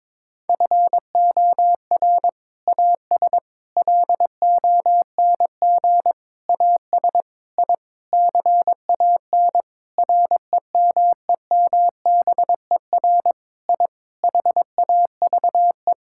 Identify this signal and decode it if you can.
Morse code, heard as a keyed tone, about 22 wpm, 700 Hz. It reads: FOR AS LONG AS I CAN REMEMBER I HAVE
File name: practice / 22wpm / Sentences with 8-Word Phrases - 22wpm